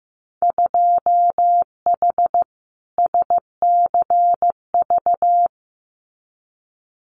Morse code, heard as a keyed tone, about 15 wpm, 700 Hz. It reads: 2H SCV
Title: Winter Field Day Exchanges Encore 15wpm